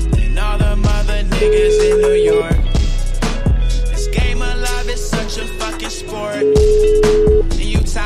Ringback tone (Europe, 425 Hz), with noise.